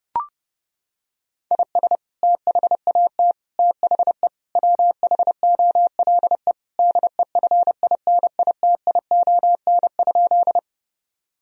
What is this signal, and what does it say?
Morse code, heard as a keyed tone, about 32 wpm, 700 Hz. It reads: IS THAT THE WHOLE DEFINITION?
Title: Sentences from Top 1000 Words - Review Part 4 - 30wpm